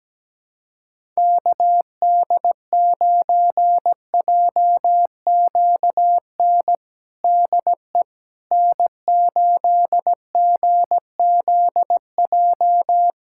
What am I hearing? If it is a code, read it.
Morse code, heard as a keyed tone, about 17 wpm, 700 Hz. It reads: KD9JQN DE N8GZJ
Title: K1USN SST QSO Practice 17wpm